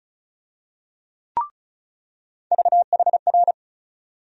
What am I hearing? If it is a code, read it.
Morse code, heard as a keyed tone, about 34 wpm, 700 Hz. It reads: VHR
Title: USA Call Sign Postfixes - 3 Letters 35wpm